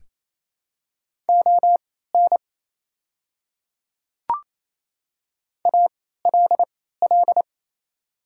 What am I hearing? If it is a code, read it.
Morse code, heard as a keyed tone, about 27 wpm, spaced as for 16 wpm, 700 Hz. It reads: ON ALL
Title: 3x Character Spacing - Top 100 Words - 28wpm